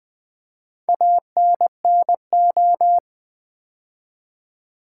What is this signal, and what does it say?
Morse code, heard as a keyed tone, about 20 wpm, 700 Hz. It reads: ANNO
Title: Build A Word - 2 Letters At A Time - 8-Letter Words - 20wpm